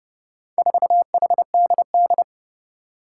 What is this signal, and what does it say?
Morse code, heard as a keyed tone, about 29 wpm, 700 Hz. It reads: 4HDD
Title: Four Challenging Letter-Number Combinations 30wpm